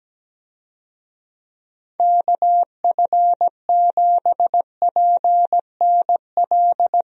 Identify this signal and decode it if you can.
Morse code, heard as a keyed tone, about 17 wpm, 700 Hz. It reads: KF7PNL
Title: USA Call Signs - 6 Characters - Repeated 2x - 17wpm